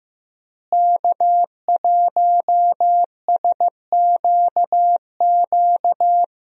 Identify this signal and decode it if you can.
Morse code, heard as a keyed tone, about 15 wpm, 700 Hz. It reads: K1SQQ
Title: USA Call Signs - 5 Characters - Repeated 3x - 15wpm